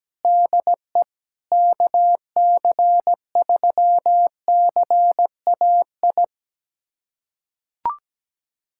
Morse code, heard as a keyed tone, about 17 wpm, 700 Hz. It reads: DE KC3CAI
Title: USA Callsign DE Callsign - 6 Characters - 17wpm